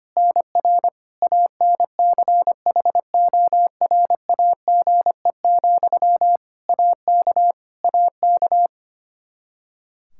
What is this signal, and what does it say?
Morse code, heard as a keyed tone, about 25 wpm, 700 Hz. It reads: NR ANCHORAGE, AK AK
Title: Practice QSO - Exchange 1 - Beginning - 25wpm